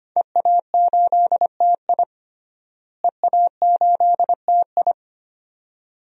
Morse code, heard as a keyed tone, about 25 wpm, 700 Hz. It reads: EA8TI EA8TI
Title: DXCC-100 Call Signs - 5 Characters - Repeated 2x - 25wpm